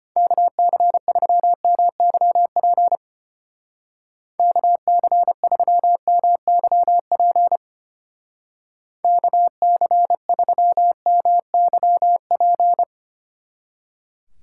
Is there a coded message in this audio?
Yes: Morse code from 0.2 to 12.8 s, heard as a keyed tone, about 32 wpm, 700 Hz. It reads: KC3MYP KC3MYP KC3MYP